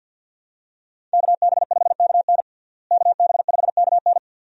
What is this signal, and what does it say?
Morse code, heard as a keyed tone, about 52 wpm, 700 Hz. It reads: KB5KN KB5KN
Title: USA Call Signs - 5 Characters - Repeated 2x - 50wpm